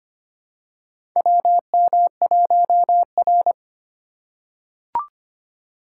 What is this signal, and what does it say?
Morse code, heard as a keyed tone, about 25 wpm, 700 Hz. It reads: WM1R
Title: USA Call Signs - 4 Characters - 2x Character Spacing - 25wpm